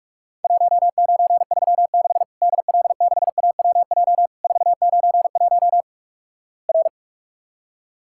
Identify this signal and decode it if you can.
Morse code, heard as a keyed tone, about 46 wpm, 700 Hz. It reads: 1936 DL6AWJ 491 R
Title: Worked All Europe DX Contest - QTC Exchange - 45wpm